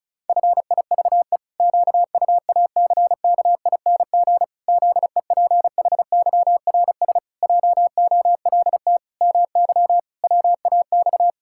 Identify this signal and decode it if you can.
Morse code, heard as a keyed tone, about 36 wpm, 700 Hz. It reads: FIVE QUACKING ZEPHYRS JOLT MY WAX
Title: Pangram Challenge v2 - 35wpm